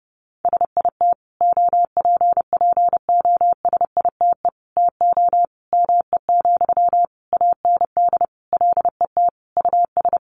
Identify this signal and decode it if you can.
Morse code, heard as a keyed tone, about 28 wpm, 700 Hz. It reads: SIT OPPOSITE TO ME, AND LET US
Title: Sentences from Top 1300 Words 30wpm